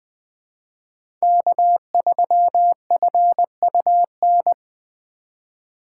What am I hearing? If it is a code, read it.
Morse code, heard as a keyed tone, about 20 wpm, 700 Hz. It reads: K3FUN